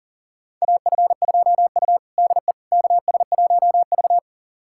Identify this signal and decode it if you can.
Morse code, heard as a keyed tone, about 37 wpm, 700 Hz. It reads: AF2U DE KS1V
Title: USA Callsign DE Callsign - 4 Characters - 40wpm